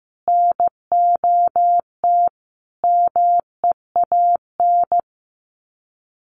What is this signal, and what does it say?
Morse code, heard as a keyed tone, about 15 wpm, 700 Hz. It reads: NOT MEAN
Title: Sets of 5 Words 15wpm